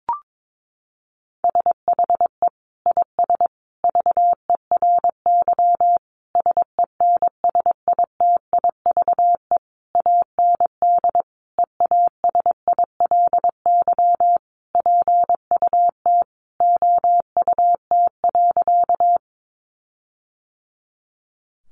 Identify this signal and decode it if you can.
Morse code, heard as a keyed tone, about 22 wpm, 700 Hz. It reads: SHE IS VERY SENSITIVE AND EASILY PUT OUT.
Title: Sentences from Top 1400 Words 22wpm